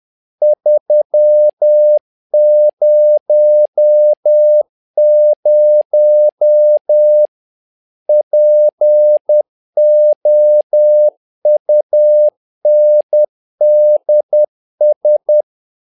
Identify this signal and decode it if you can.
Morse code, heard as a keyed tone, about 10 wpm, 600 Hz. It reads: 300 POUNDS
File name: learning center / tower trouble / tower trouble 10wpm